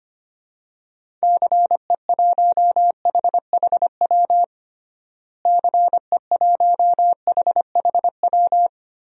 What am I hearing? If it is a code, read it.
Morse code, heard as a keyed tone, about 25 wpm, 700 Hz. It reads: CE1HHW CE1HHW
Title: DXCC-100 Call Signs - 6 Characters - Repeated 3x - 25wpm